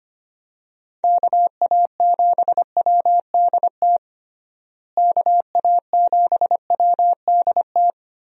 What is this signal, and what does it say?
Morse code, heard as a keyed tone, about 25 wpm, 700 Hz. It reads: KA7WDT KA7WDT
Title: USA Call Signs - 6 Characters - Repeated 3x - 25wpm